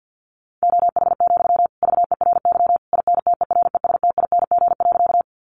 Morse code, heard as a keyed tone, about 50 wpm, 700 Hz. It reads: OH, VERY INTERESTING.
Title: Sentences from Top 1000 Words 50wpm